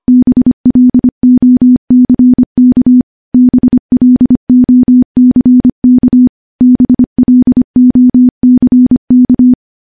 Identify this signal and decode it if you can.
Morse code, heard as a keyed tone, about 25 wpm, 250 Hz. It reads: BLOCK BLOCK BLOCK
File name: morse / morse-training-25WPM